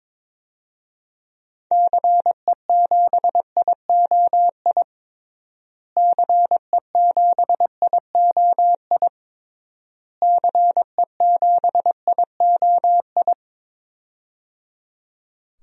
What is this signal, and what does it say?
Morse code, heard as a keyed tone, about 22 wpm, 700 Hz. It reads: CE7IOI CE7IOI CE7IOI
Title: DXCC-100 Call Signs - 6 Characters - Repeated 3x - 22wpm